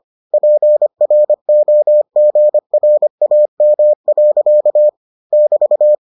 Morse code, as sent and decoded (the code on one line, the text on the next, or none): .--. .-. --- --. .-. .- -- .-.-.- / -...-
PROGRAM. <BT>